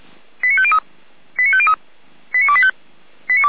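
Five-tone selective call (ZVEI), heard as quick runs of five tones, tones of about 80 ms. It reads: 89503 89503 89285
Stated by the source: A ZVEI five-tone selective call received by SDR, logged on the Feuerwehr Orlamünde audio server.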